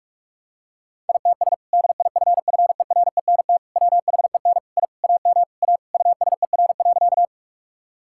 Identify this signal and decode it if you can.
Morse code, heard as a keyed tone, about 41 wpm, 700 Hz. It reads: ITS DIFFERENT WHEN I AM A USER.
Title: Sentences from Top 1000 Words - Review Part 5 - 45wpm